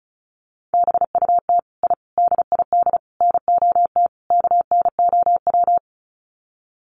Morse code, heard as a keyed tone, about 34 wpm, 700 Hz. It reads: BUT I DID NOT KNOW
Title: Sets of 5 Words Encore 35wpm